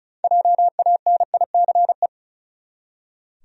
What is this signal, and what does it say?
Morse code, heard as a keyed tone, about 34 wpm, 700 Hz. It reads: JANICE
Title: USA Common Names - 35wpm